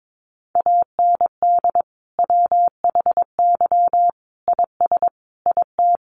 Morse code, heard as a keyed tone, about 22 wpm, 700 Hz. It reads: AND WHY IS IT
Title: Sentences from Top 1100 Words 22wpm